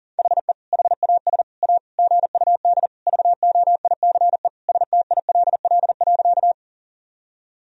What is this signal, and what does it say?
Morse code, heard as a keyed tone, about 40 wpm, 700 Hz. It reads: HE HAS A GUD VOICE STILL.
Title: Sentences from Top 500 Words and Abbreviations 40wpm